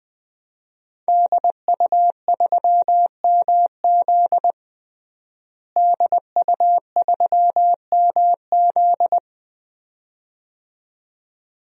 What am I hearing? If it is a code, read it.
Morse code, heard as a keyed tone, about 20 wpm, 700 Hz. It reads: DU3MZ DU3MZ
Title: DXCC-100 Call Signs - 5 Characters - Repeated 2x - 20wpm